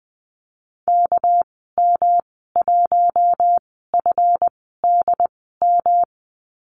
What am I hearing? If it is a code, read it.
Morse code, heard as a keyed tone, about 20 wpm, spaced as for 14 wpm, 700 Hz. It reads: KM1FDM
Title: 200 Lesson 32g - Letter k - Call Signs 20wpm